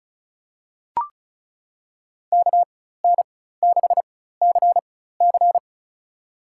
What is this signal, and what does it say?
Morse code, heard as a keyed tone, about 36 wpm, spaced as for 16 wpm, 700 Hz. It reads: KN6CC
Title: USA Call Signs - 5 Characters - 4x Character Spacing - 35wpm